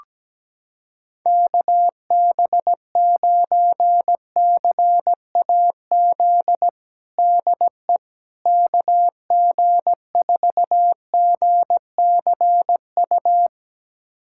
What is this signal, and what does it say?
Morse code, heard as a keyed tone, about 17 wpm, 700 Hz. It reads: KB9CAZ DE KG4GCU
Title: USA Callsign DE Callsign 17wpm